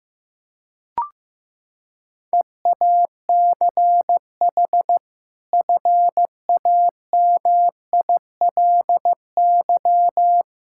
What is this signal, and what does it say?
Morse code, heard as a keyed tone, about 15 wpm, 700 Hz. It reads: EACH FAMILY
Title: Sentences from Top 1000 Words - Review Part 5 - 15wpm